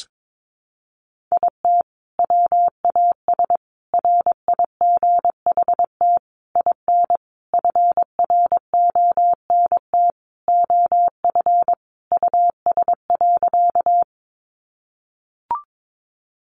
Morse code, heard as a keyed tone, about 22 wpm, 700 Hz. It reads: IT WAS RIGHT IN FRONT OF US.
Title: Sentences from Top 500 Words 22wpm